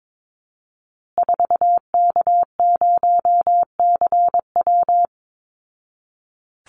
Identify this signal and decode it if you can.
Morse code, heard as a keyed tone, about 22 wpm, 690 Hz. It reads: 4K0CW
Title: Call Signs Repeated 3x - 22wpm